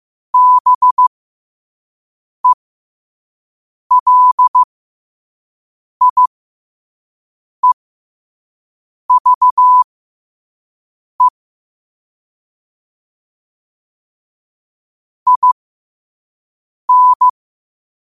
Morse code, heard as a keyed tone, about 15 wpm, spaced as for 5 wpm, 1000 Hz. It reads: BELIEVE IN
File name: qotdfiles / 5wpm